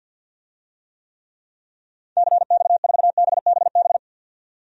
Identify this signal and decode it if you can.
Morse code, heard as a keyed tone, about 50 wpm, 700 Hz. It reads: CX4BBB
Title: DXCC-100 Call Signs - 6 Characters - 50wpm